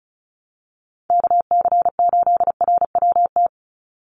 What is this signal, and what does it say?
Morse code, heard as a keyed tone, about 34 wpm, 700 Hz. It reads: KC8RWT
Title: USA Call Signs 35wpm